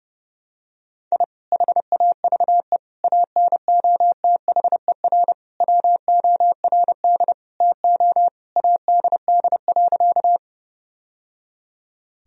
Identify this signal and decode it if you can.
Morse code, heard as a keyed tone, about 30 wpm, 700 Hz. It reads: I HAVE ANOTHER WORD TO ADD.
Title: Sentences from Top 1000 Words - Review Part 1 - 30wpm